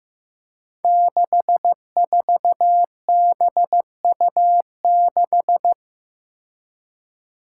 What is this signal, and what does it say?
Morse code, heard as a keyed tone, about 16 wpm, 700 Hz. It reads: 64BU6
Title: Five Challenging Letter-Number Combinations 15wpm